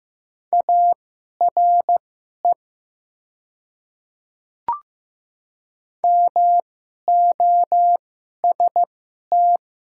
Morse code, heard as a keyed tone, about 15 wpm, spaced as for 11 wpm, 690 Hz. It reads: ARE MOST
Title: 2x Character Spacing - Top 100 Words - 15wpm